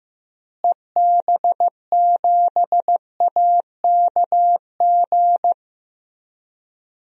Morse code, heard as a keyed tone, about 15 wpm, 700 Hz. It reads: EB7AKG
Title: DXCC-100 Call Signs - 6 Characters - Repeated 3x - 15wpm